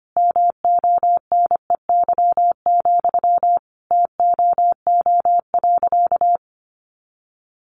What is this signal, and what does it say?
Morse code, heard as a keyed tone, about 25 wpm, 700 Hz. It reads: MONEY, TOO.
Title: Sentences from Top 900 Words 25wpm